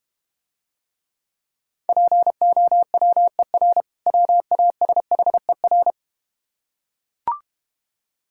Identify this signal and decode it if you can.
Morse code, heard as a keyed tone, about 32 wpm, 700 Hz. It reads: POWER WASHER